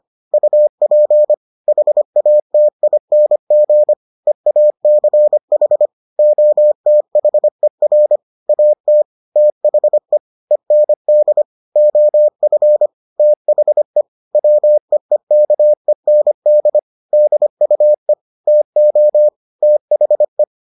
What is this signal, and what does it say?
Morse code, heard as a keyed tone, about 25 wpm, 600 Hz. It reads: UP HATING EACH OTHER AT THE END OF THE WEEKEND DUE TO THE